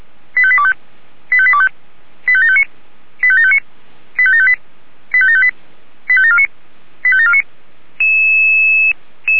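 Five-tone selective call (ZVEI), heard as quick runs of five tones, tones of about 70 ms. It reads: 85825 85825 85759 85759 85758 85758 85749 85749
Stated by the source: ZVEI five-tone selective call 85729 received with an SDR.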